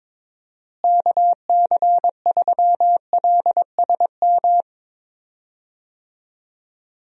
Morse code, heard as a keyed tone, about 21 wpm, 700 Hz. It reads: KC3LSM
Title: USA Call Signs - 6 Characters - Repeated 2x - 22wpm